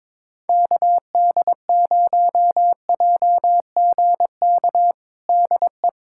Morse code, heard as a keyed tone, about 22 wpm, 700 Hz. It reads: KD0JGK DE
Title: USA Callsign DE Callsign - 6 Characters - 22wpm